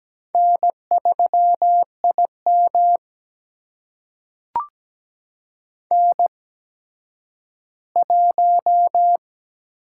Morse code, heard as a keyed tone, about 17 wpm, spaced as for 10 wpm, 700 Hz. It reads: N3IM N1
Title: USA Call Signs - 4 Characters - 8x Character Spacing - 17wpm